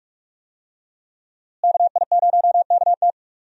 Morse code, heard as a keyed tone, about 46 wpm, 700 Hz. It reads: KI0KT
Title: USA Call Signs - 5 Characters - Repeated 3x - 45wpm